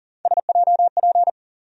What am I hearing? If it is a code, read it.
Morse code, heard as a keyed tone, about 40 wpm, 700 Hz. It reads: SJP
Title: USA Call Sign Postfixes - 3 Letters 40wpm